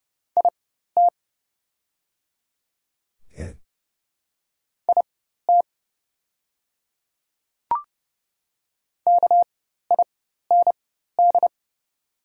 Morse code, heard as a keyed tone, about 30 wpm, spaced as for 14 wpm, 700 Hz. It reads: IT IT KIND
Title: 4x Character Spacing - Top 200 Words - 30wpm